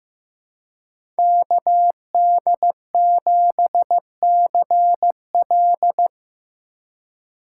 Morse code, heard as a keyed tone, about 15 wpm, 700 Hz. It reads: KD7CL
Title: USA Call Signs - 5 Characters - 4x Character Spacing - 15wpm